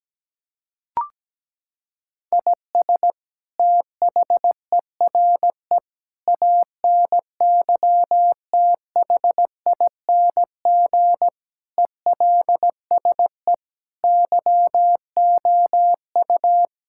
Morse code, heard as a keyed tone, about 17 wpm, 700 Hz. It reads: IS THERE ANYTHING ELSE YOU
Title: Sentences from Top 1600 Words 17wpm